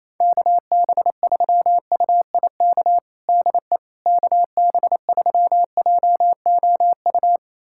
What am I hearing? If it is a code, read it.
Morse code, heard as a keyed tone, about 28 wpm, 700 Hz. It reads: KB3UIK DE KB3JOU
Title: USA Callsign DE Callsign - 6 Characters - 28wpm